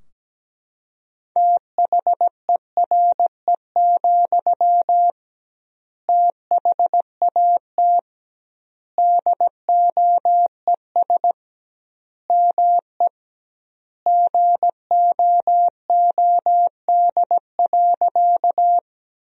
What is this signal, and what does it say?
Morse code, heard as a keyed tone, about 17 wpm, 700 Hz. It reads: THERE, THAT DOES ME GOOD.
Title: 2x Word-Spacing - Sentences from Top 200 Words - 17wpm